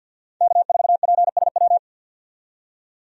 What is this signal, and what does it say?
Morse code, heard as a keyed tone, about 50 wpm, 700 Hz. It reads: K4PSW
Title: USA Call Signs - 5 Characters - 50wpm